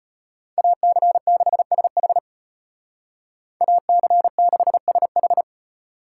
Morse code, heard as a keyed tone, about 33 wpm, 700 Hz. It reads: AC6SH AC6SH